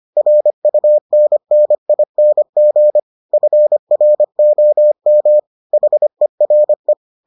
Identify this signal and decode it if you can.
Morse code, heard as a keyed tone, about 25 wpm, 600 Hz. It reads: RUNNING FROM HERE